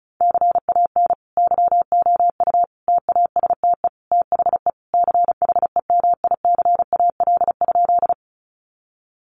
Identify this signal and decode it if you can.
Morse code, heard as a keyed tone, about 34 wpm, 700 Hz. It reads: CAN YOU TASTE THE CHEMICAL?
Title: Sentences from Top 1200 Words 35wpm